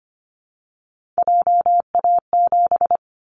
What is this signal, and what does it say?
Morse code, heard as a keyed tone, about 25 wpm, 690 Hz. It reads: JA7
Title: Callsign Prefixes for Top 50 DXCC - 25wpm